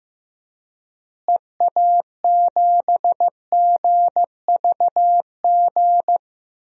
Morse code, heard as a keyed tone, about 15 wpm, 700 Hz. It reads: EA7GVG
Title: DXCC-100 Call Signs - 6 Characters - Repeated 3x - 15wpm